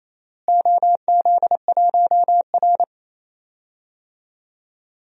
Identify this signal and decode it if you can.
Morse code, heard as a keyed tone, about 27 wpm, 700 Hz. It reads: OZ1R
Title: DXCC-100 Call Signs - 4 Characters - Repeated 3x - 28wpm